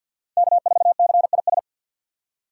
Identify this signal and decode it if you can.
Morse code, heard as a keyed tone, about 49 wpm, 700 Hz. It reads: K4CIS
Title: USA Call Signs - 5 Characters - 6x Character Spacing - 50wpm